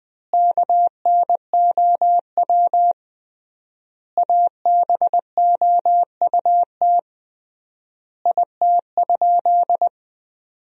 Morse code, heard as a keyed tone, about 21 wpm, 700 Hz. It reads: KNOW ABOUT IT?
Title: 3x Word-Spacing - Sentences from Top 100 Words - 20wpm